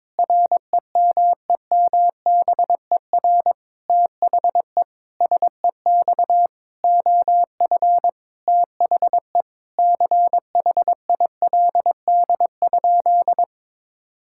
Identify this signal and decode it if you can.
Morse code, heard as a keyed tone, about 22 wpm, 700 Hz. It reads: REMEMBER THE SEX OF THE CHILD?
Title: Sentences from Top 1600 Words 22wpm